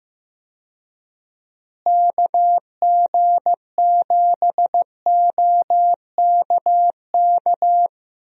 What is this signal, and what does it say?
Morse code, heard as a keyed tone, about 15 wpm, 700 Hz. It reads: KG7OKK
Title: USA Call Signs - 6 Characters - Repeated 2x - 15wpm